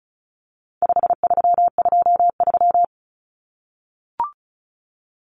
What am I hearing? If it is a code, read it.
Morse code, heard as a keyed tone, about 34 wpm, 700 Hz. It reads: 5323
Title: Numbers - 4-Digit 35wpm